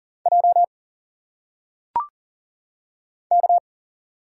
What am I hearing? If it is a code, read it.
Morse code, heard as a keyed tone, about 39 wpm, 700 Hz. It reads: J K
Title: Single Letters - Phonetic Pronunciation 40wpm